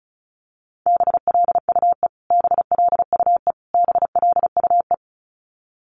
Morse code, heard as a keyed tone, about 34 wpm, 700 Hz. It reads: BLUE BLUE BLUE
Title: Top 500 Words Repeated 35wpm